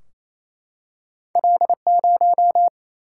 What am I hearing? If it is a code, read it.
Morse code, heard as a keyed tone, about 28 wpm, 700 Hz. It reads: L0
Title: Single-Letter Single-Number 28wpm